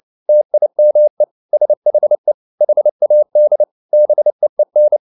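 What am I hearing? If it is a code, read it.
Morse code, heard as a keyed tone, about 29 wpm, 600 Hz. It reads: TIME SHE HAD BEEN
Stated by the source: Morse code at 29 wpm, sending Ken Follett's novel Never in English, from CW med Gnister.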